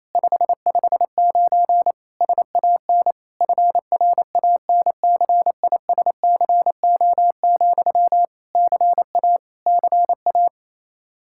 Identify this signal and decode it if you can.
Morse code, heard as a keyed tone, about 28 wpm, 700 Hz. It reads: 559 SAN FRANCISCO, CA CA
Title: Practice QSO - Exchange 1 - Beginning - 28wpm